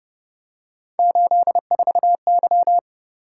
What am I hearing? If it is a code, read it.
Morse code, heard as a keyed tone, about 31 wpm, 700 Hz. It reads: 84Y